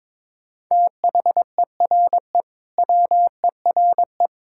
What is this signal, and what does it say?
Morse code, heard as a keyed tone, about 22 wpm, 700 Hz. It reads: THERE WERE
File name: practice / 22wpm / Sentences with 4-Word Phrases - 22wpm